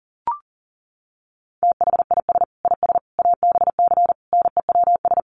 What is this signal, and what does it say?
Morse code, heard as a keyed tone, about 40 wpm, 700 Hz. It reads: THIS IS ABC NEWS
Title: Sets of 5 Words 40wpm